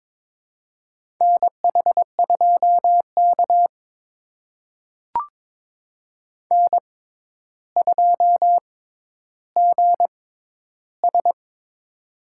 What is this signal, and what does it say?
Morse code, heard as a keyed tone, about 22 wpm, spaced as for 11 wpm, 700 Hz. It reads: NH2K N2GS
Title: USA Call Signs - 4 Characters - 6x Character Spacing - 22wpm